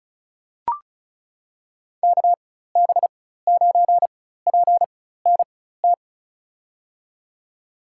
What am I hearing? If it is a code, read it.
Morse code, heard as a keyed tone, about 36 wpm, spaced as for 16 wpm, 700 Hz. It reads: KB9PNT
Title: USA Call Signs - 6 Characters - 4x Character Spacing - 35wpm